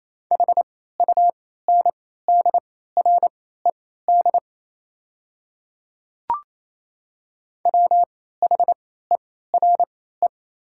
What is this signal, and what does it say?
Morse code, heard as a keyed tone, about 28 wpm, spaced as for 16 wpm, 700 Hz. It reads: HUNDRED WHERE
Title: 3x Character Spacing - Top 500 Words - 28wpm